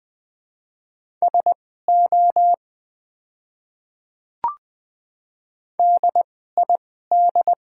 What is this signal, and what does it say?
Morse code, heard as a keyed tone, about 20 wpm, spaced as for 14 wpm, 700 Hz. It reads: SO DID
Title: 2x Character Spacing - Top 300 Words - 20wpm